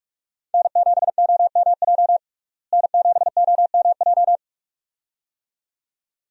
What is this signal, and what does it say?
Morse code, heard as a keyed tone, about 46 wpm, 700 Hz. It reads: N7OMJ N7OMJ
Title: USA Call Signs - 5 Characters - Repeated 2x - 45wpm